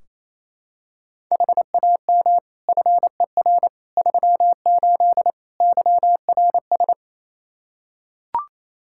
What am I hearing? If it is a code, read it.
Morse code, heard as a keyed tone, about 28 wpm, 700 Hz. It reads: HAM FER 38 YRS
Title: K1USN SST QSO Practice 28wpm